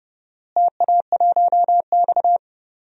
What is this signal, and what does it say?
Morse code, heard as a keyed tone, about 31 wpm, 700 Hz. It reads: TA1X
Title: DXCC-100 Call Signs - 4 Characters - Repeated 2x - 30wpm